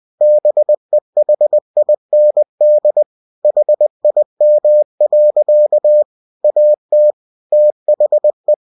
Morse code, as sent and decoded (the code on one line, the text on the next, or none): -... .... .. -. -.. / .... .. -- .-.-.- / .- - / - .... .
BEHIND HIM. AT THE